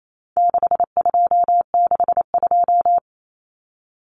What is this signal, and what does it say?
Morse code, heard as a keyed tone, about 28 wpm, 700 Hz. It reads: 6262
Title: Numbers - 4-Digit 28wpm